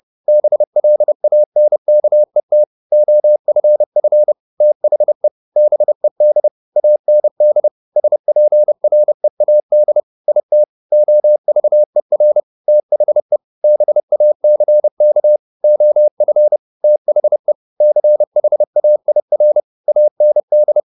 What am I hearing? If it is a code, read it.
Morse code, heard as a keyed tone, about 30 wpm, 600 Hz. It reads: BLANKET OFF THE BED AND SPREAD IT OVER THE BACK OF THE CHAIR AND
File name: the old man and the sea / the old man and the sea 1 30wpm